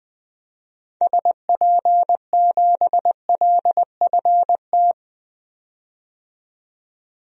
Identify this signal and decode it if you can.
Morse code, heard as a keyed tone, about 20 wpm, 700 Hz. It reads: SP7LFT
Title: DXCC-100 Call Signs - 6 Characters - 20wpm